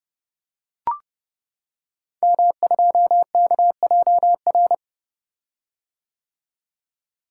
Morse code, heard as a keyed tone, about 30 wpm, 700 Hz. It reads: M2KJR
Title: Code Groups - 5 Letter:Numbers - 30wpm